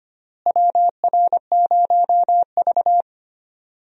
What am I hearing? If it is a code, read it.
Morse code, heard as a keyed tone, about 25 wpm, 700 Hz. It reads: WR0V
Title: USA Call Signs - 4 Characters - Repeated 2x - 25wpm